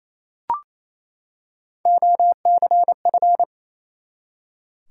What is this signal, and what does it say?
Morse code, heard as a keyed tone, about 28 wpm, 700 Hz. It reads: OCF
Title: Top 180 QSO Elements 28wpm